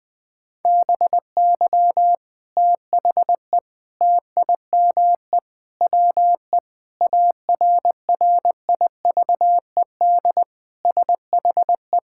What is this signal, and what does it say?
Morse code, heard as a keyed tone, about 20 wpm, 700 Hz. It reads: BY THE TIME WE ARRIVED SHE
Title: Sentences with 3-Word Phrases - Spoken Beginning - 20wpm